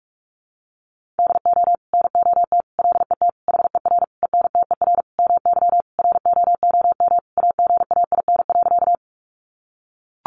Morse code, heard as a keyed tone, about 44 wpm, 700 Hz. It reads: DO NOT LET HER ENTER MY ROOM AGAIN.
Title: Sentences from Top 1300 Words 45wpm